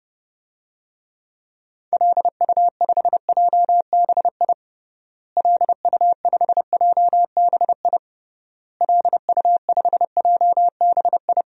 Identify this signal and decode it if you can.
Morse code, heard as a keyed tone, about 31 wpm, 700 Hz. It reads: LU5JBI LU5JBI LU5JBI
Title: DXCC-100 Call Signs - 6 Characters - Repeated 3x - 30wpm